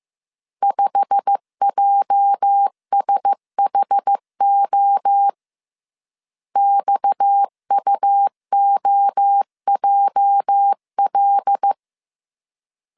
Morse code, heard as a keyed tone, about 15 wpm, 800 Hz. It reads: 5JSHO XUOJL
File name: CW / DET / 1 9 1500C S A 14wpm 3222